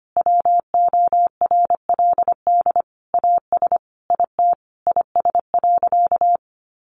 Morse code, heard as a keyed tone, about 25 wpm, 700 Hz. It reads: WORLD AS IT IS.
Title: Sentences from Top 1200 Words 25wpm